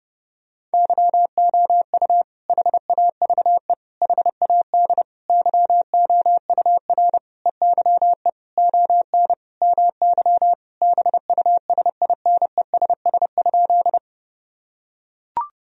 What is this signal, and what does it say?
Morse code, heard as a keyed tone, about 30 wpm, 700 Hz. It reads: YOU HAVE HAD YOUR EYE ON MY BUSINESS?
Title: Sentences from Top 1000 Words - Review Part 1 - 30wpm